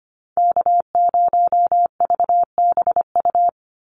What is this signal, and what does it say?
Morse code, heard as a keyed tone, about 25 wpm, 700 Hz. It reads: K0VBU
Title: Call Signs - 25wpm